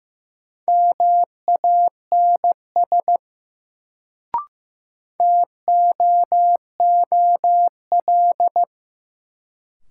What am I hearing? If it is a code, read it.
Morse code, heard as a keyed tone, about 15 wpm, 700 Hz. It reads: MANS TOOL
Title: Sentences from Top 1000 Words - Review Part 6 - New Word Drill - 15wpm